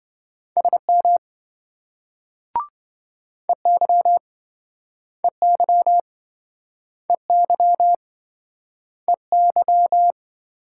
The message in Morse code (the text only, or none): SM EY EY EY EY